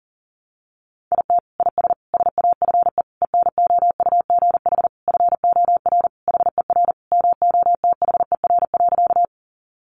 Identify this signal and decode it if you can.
Morse code, heard as a keyed tone, about 40 wpm, 700 Hz. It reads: IT IS SAFE ENOUGH FOR HER MOTHER.
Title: Sentences from Top 900 Words 40wpm